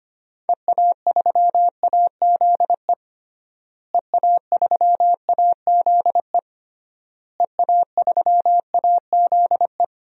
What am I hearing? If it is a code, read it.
Morse code, heard as a keyed tone, about 25 wpm, 700 Hz. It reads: EA3AZE EA3AZE EA3AZE
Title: DXCC-100 Call Signs - 6 Characters - Repeated 3x - 25wpm